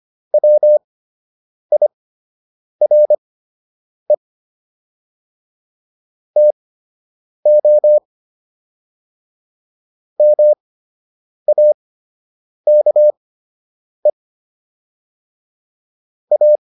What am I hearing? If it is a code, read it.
Morse code, heard as a keyed tone, about 25 wpm, spaced as for 8 wpm, 600 Hz. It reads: WIRE TO MAKE A